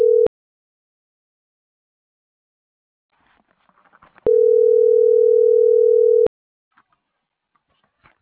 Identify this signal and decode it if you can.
Ringback tone (North America), 440 + 480 Hz.